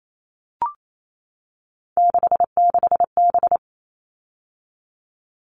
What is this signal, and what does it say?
Morse code, heard as a keyed tone, about 28 wpm, 700 Hz. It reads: 66B
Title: All Cut Numbers - 3-Digit 28wpm